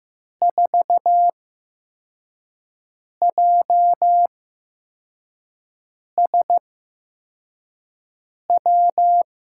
Morse code, heard as a keyed tone, about 15 wpm, spaced as for 4 wpm, 700 Hz. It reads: 4JSW
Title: USA Call Signs - 8x Character Spacing - 15wpm